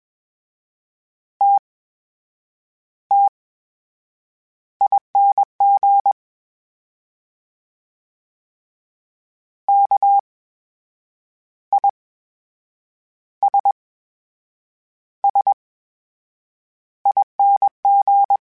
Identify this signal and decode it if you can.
Morse code, heard as a keyed tone, about 21 wpm, spaced as for 7 wpm, 800 Hz. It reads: TTING KISSING
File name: Special INGs-mp3s / Special INGs-mp3s / VerySlowToFaster Ing @20